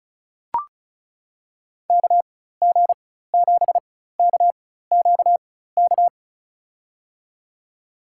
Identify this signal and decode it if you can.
Morse code, heard as a keyed tone, about 36 wpm, spaced as for 16 wpm, 700 Hz. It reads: KG7KQK
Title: USA Call Signs - 6 Characters - 4x Character Spacing - 35wpm